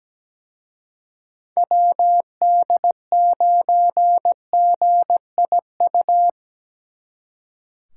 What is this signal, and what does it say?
Morse code, heard as a keyed tone, about 20 wpm, 700 Hz. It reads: WD9GIU
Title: Speed Racing - USA 2x3 Call Signs 17wpm